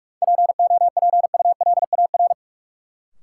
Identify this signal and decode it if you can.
Morse code, heard as a keyed tone, about 43 wpm, 700 Hz. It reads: POPULAR
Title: Sentences from Top 1200 Words - New Word Drill Alt - 45wpm